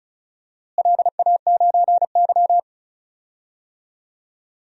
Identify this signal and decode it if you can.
Morse code, heard as a keyed tone, about 36 wpm, 700 Hz. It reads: LA9Y
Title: DXCC-100 Call Signs - 4 Characters - Repeated 2x - 35wpm